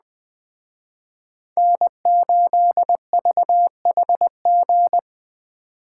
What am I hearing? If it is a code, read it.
Morse code, heard as a keyed tone, about 19 wpm, 700 Hz. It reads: N8VHG